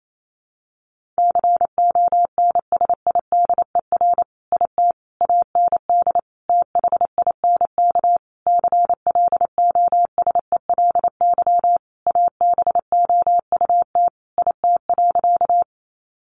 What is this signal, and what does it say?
Morse code, heard as a keyed tone, about 28 wpm, 700 Hz. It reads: CONSIDER IT AND THINK CLOSELY ABOUT IT.
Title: Sentences from Top 1400 Words 28wpm